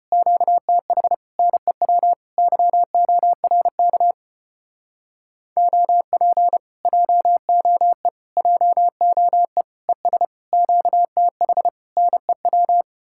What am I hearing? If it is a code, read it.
Morse code, heard as a keyed tone, about 33 wpm, 700 Hz. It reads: QTH NEW YORK OP JOE JOE ES QTH NEW